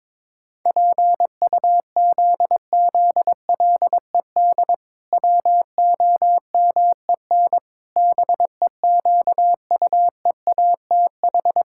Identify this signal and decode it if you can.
Morse code, heard as a keyed tone, about 22 wpm, 700 Hz. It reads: PUZZLED WOMEN BEQUEATH
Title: Pangram Challenge v2 - 22wpm